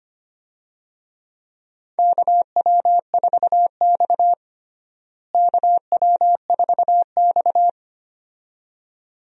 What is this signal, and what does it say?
Morse code, heard as a keyed tone, about 25 wpm, 700 Hz. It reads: KW4X KW4X
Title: USA Call Signs - 4 Characters - Repeated 2x - 25wpm